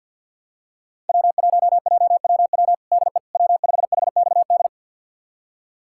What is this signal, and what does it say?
Morse code, heard as a keyed tone, about 49 wpm, 700 Hz. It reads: W1JWW DE W5HXD
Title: USA Callsign DE Callsign 50wpm